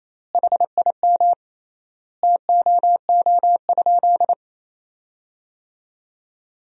Morse code, heard as a keyed tone, about 28 wpm, 700 Hz. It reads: HIM TOO?
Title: 3x Word-Spacing - Sentences from Top 500 Words - 28wpm